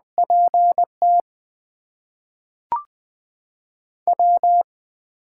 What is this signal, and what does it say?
Morse code, heard as a keyed tone, about 20 wpm, spaced as for 5 wpm, 700 Hz. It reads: PTW
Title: USA Call Signs - 4 Characters - 8x Character Spacing - 20wpm